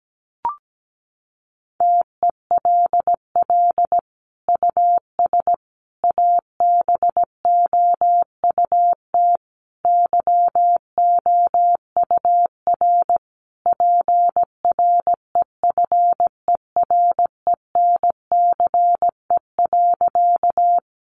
Morse code, heard as a keyed tone, about 17 wpm, 700 Hz. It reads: TELL US ABOUT YOUR PREFERENCE.
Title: Sentences from Top 1500 Words 17wpm